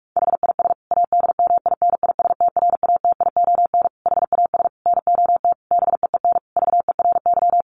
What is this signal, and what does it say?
Morse code, heard as a keyed tone, about 44 wpm, 700 Hz. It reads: HIS ADMINISTRATION HAS NOT BEEN VERY
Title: Sentences from Top 1300 Words 45wpm